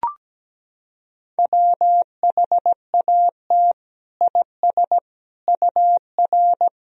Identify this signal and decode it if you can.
Morse code, heard as a keyed tone, about 17 wpm, 700 Hz. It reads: WHAT IS UR
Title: Sentences from Top 300 Words and Abbreviations 17wpm